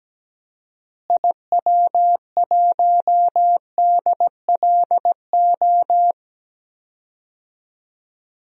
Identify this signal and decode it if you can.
Morse code, heard as a keyed tone, about 17 wpm, 700 Hz. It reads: IW1DLO
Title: DXCC-100 Call Signs - 6 Characters - Repeated 2x - 17wpm